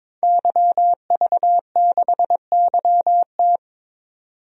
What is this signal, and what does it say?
Morse code, heard as a keyed tone, about 22 wpm, 700 Hz. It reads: YV6YT
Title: DXCC-100 Call Signs - 5 Characters - 22wpm